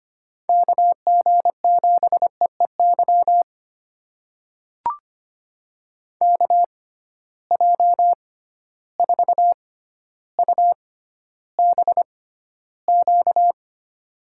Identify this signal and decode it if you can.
Morse code, heard as a keyed tone, about 25 wpm, spaced as for 13 wpm, 700 Hz. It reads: KG7EEY KJ4UBQ
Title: USA Call Signs - 6 Characters - 6x Character Spacing - 25wpm